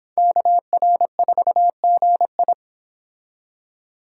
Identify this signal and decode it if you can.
Morse code, heard as a keyed tone, about 26 wpm, 700 Hz. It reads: KR4GI